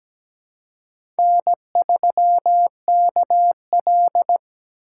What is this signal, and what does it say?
Morse code, heard as a keyed tone, about 17 wpm, 700 Hz. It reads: N3KL
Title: USA Call Signs - 3x Character Spacing - 17wpm